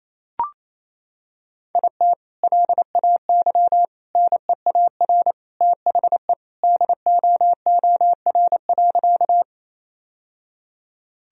Morse code, heard as a keyed tone, about 28 wpm, 700 Hz. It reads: IT LAY NEAR THE DOOR.
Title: Sentences from Top 1000 Words - Review Part 2 - 28wpm